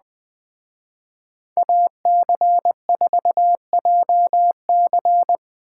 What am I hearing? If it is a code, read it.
Morse code, heard as a keyed tone, about 20 wpm, 700 Hz. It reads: AC4JC
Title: Speed Racing - USA Call Signs 17wpm